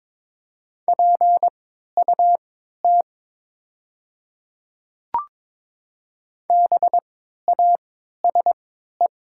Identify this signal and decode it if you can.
Morse code, heard as a keyed tone, about 22 wpm, spaced as for 12 wpm, 700 Hz. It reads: PUT BASE
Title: 3x Character Spacing - Top 300 Words - 22wpm